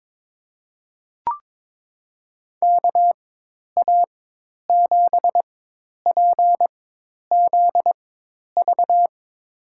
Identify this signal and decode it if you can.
Morse code, heard as a keyed tone, about 22 wpm, spaced as for 10 wpm, 700 Hz. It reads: KA7PZV
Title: USA Call Signs - 6 Characters - 4x Character Spacing - 22wpm